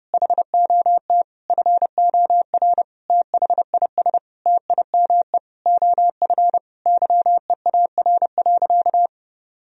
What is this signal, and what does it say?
Morse code, heard as a keyed tone, about 30 wpm, 700 Hz. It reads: HOT FOR THIS TIME OF YEAR.
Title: Sentences from Top 1000 Words - Review Part 3 - 30wpm